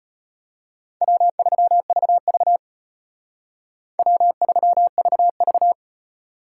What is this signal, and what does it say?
Morse code, heard as a keyed tone, about 32 wpm, 700 Hz. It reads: W3VV W3VV